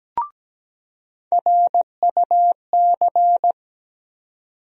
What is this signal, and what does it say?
Morse code, heard as a keyed tone, about 17 wpm, 700 Hz. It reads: RUC
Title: USA Call Sign Postfixes - 3 Letters 17wpm